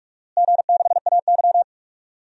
Morse code, heard as a keyed tone, about 46 wpm, 700 Hz. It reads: G6AY
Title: DXCC-100 Call Signs - 4 Characters - Repeated 3x - 45wpm